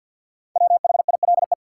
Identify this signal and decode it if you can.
Morse code, heard as a keyed tone, about 50 wpm, 700 Hz. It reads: WHILE
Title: Sentences from Top 300 Words - New Word Drill - 50wpm